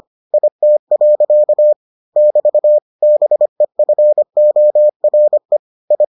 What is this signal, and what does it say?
Morse code, heard as a keyed tone, about 25 wpm, 600 Hz. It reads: IT. <BT> BEFORE I